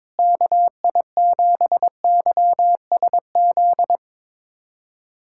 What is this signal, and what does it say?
Morse code, heard as a keyed tone, about 22 wpm, 700 Hz. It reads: KI7YSZ
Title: USA Callsign DE Callsign - 6 Characters - 22wpm